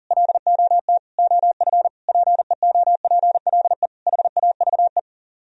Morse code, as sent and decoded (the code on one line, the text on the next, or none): .-.. --- - / --- ..-. / .--. --- .--. .-.. / .... .- ...- .
LOT OF PEOPLE HAVE